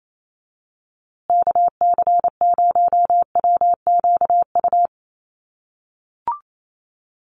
Morse code, heard as a keyed tone, about 28 wpm, 700 Hz. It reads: KC0WQU
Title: USA Call Signs Repeated 2x 28wpm